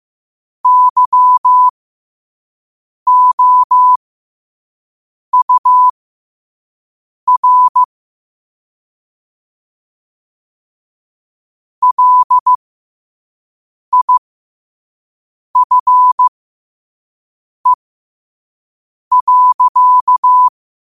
Morse code, heard as a keyed tone, about 15 wpm, spaced as for 5 wpm, 1000 Hz. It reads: YOUR LIFE.
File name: qotdfiles / 5wpm